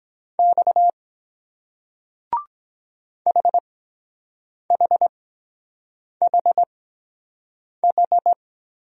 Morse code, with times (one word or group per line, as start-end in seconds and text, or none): 0.4-0.9 X
3.3-8.3 HHHH